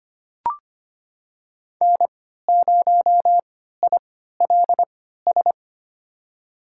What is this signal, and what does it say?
Morse code, heard as a keyed tone, about 25 wpm, spaced as for 14 wpm, 700 Hz. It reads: N0ILS
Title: USA Call Signs - 5 Characters - 3x Character Spacing - 25wpm